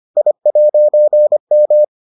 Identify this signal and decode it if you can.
Morse code, heard as a keyed tone, about 25 wpm, 600 Hz. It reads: I'M